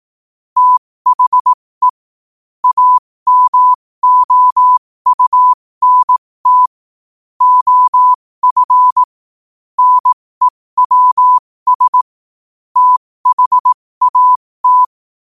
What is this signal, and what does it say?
Morse code, heard as a keyed tone, about 18 wpm, spaced as for 15 wpm, 1000 Hz. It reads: THE AMOUNT OF NEWS THAT
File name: qotdfiles / 15wpm